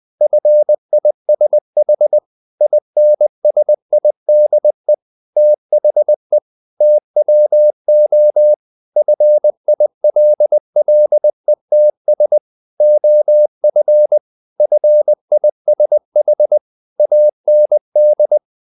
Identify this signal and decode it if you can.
Morse code, heard as a keyed tone, about 20 wpm, 600 Hz. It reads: FISH INSIDE THE TWO FILLETS OF FISH AND
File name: the old man and the sea part 2 20wpm